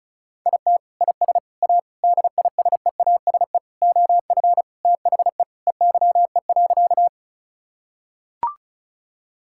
Morse code, heard as a keyed tone, about 36 wpm, 700 Hz. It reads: IT IS A DISEASE OF THE EYE.
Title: Sentences from Top 1000 Words - Review Part 6 - 35wpm